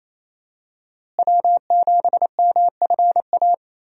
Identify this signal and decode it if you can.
Morse code, heard as a keyed tone, about 28 wpm, 700 Hz. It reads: W7MFA
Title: USA Call Signs - 5 Characters - Repeated 2x - 28wpm